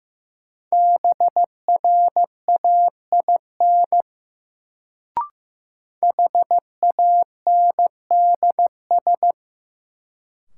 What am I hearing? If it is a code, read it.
Morse code, heard as a keyed tone, about 15 wpm, 700 Hz. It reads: BRAIN HANDS
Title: Sentences from Top 1000 Words - Review Part 6 - New Word Drill - 15wpm